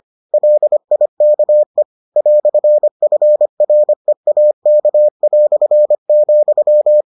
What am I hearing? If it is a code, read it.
Morse code, heard as a keyed tone, about 25 wpm, 600 Hz. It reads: LIKE "FREAK",